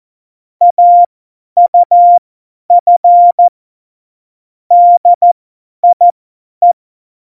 Morse code, heard as a keyed tone, about 14 wpm, spaced as for 10 wpm, 700 Hz. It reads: AUF DIE
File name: Morsekurs / Witze / Witz 06. Oktober 2020, 50 F70